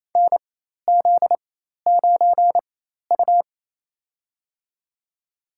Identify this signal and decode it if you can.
Morse code, heard as a keyed tone, about 28 wpm, spaced as for 13 wpm, 700 Hz. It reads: NZ9U
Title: USA Call Signs - 4 Characters - 4x Character Spacing - 28wpm